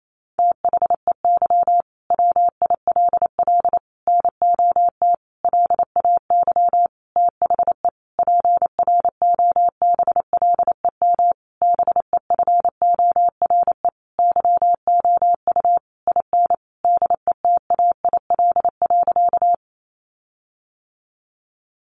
Morse code, heard as a keyed tone, about 28 wpm, 700 Hz. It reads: THEY WILL NOT LAY THE PROBLEM BEFORE YOU IN DETAIL.
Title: Sentences from Top 1500 Words 28wpm